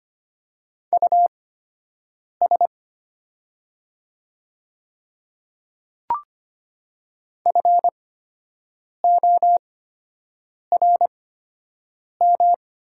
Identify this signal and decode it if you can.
Morse code, heard as a keyed tone, about 25 wpm, spaced as for 7 wpm, 700 Hz. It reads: US FORM
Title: 8x Character Spacing - Top 200 Words - 25wpm